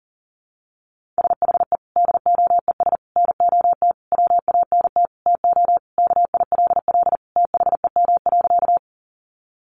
Morse code, heard as a keyed tone, about 40 wpm, 700 Hz. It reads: SHE DOES NOT WANT TO KILL THEM.
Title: Additional Easy Phrases 40wpm